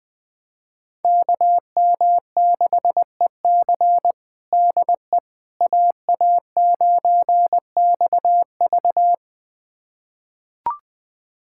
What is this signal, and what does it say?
Morse code, heard as a keyed tone, about 20 wpm, 700 Hz. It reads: KM6EC DE AA9XV
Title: USA Callsign DE Callsign - 5 Characters - 20wpm